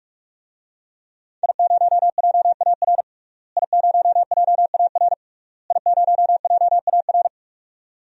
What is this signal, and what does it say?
Morse code, heard as a keyed tone, about 46 wpm, 700 Hz. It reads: I0JAR I0JAR I0JAR
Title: DXCC-100 Call Signs - 5 Characters - Repeated 3x - 45wpm